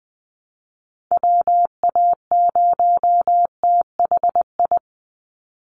Morse code, heard as a keyed tone, about 20 wpm, 700 Hz. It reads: WA0THI
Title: USA Call Signs 20wpm